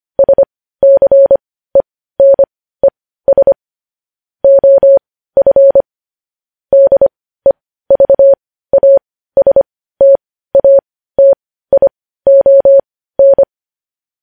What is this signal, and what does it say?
Morse code, heard as a keyed tone, about 25 wpm, spaced as for 15 wpm, 550 Hz. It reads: SCENES OF DEVASTATION